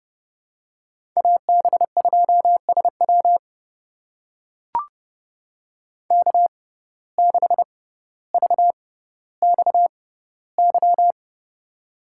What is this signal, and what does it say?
Morse code, heard as a keyed tone, about 30 wpm, spaced as for 15 wpm, 700 Hz. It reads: AB2SW K6VXY
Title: USA Call Signs - 6x Character Spacing - 30wpm